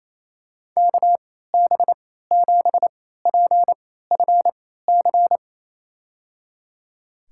Morse code, heard as a keyed tone, about 28 wpm, spaced as for 16 wpm, 690 Hz. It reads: KB7PFC
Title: USA Call Signs - 3x Character Spacing - 28wpm